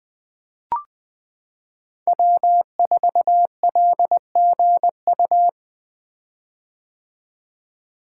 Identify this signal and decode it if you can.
Morse code, heard as a keyed tone, about 20 wpm, 700 Hz. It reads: W4LGU
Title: Code Groups - 5 Letter:Numbers - 20wpm